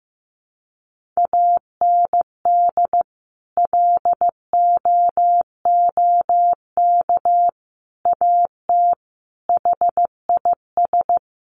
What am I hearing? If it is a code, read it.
Morse code, heard as a keyed tone, about 15 wpm, 700 Hz. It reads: AND LOOK AT HIS
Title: Sentences from Top 1500 Words 15wpm